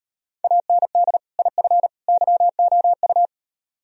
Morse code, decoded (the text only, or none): AND IF YOU